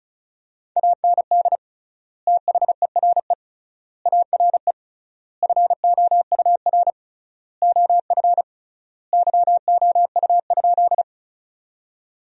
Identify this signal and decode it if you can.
Morse code, heard as a keyed tone, about 36 wpm, 700 Hz. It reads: AND THERE ARE FOUR OF YOU?
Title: 3x Word-Spacing - Sentences from Top 300 Words - 35wpm